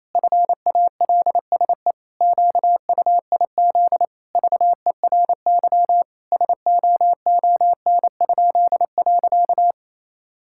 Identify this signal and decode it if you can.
Morse code, heard as a keyed tone, about 28 wpm, 700 Hz. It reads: FALSE QUIZ VERY SOON?.
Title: Pangram Challenge v2 - 28wpm